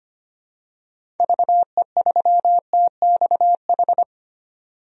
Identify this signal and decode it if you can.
Morse code, heard as a keyed tone, about 25 wpm, 700 Hz. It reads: VE3TXH
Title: DXCC-100 Call Signs - 6 Characters - 25wpm